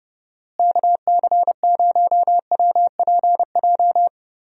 Morse code, heard as a keyed tone, about 31 wpm, 700 Hz. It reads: KC0WPJ
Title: USA Call Signs - 6 Characters - Repeated 2x - 30wpm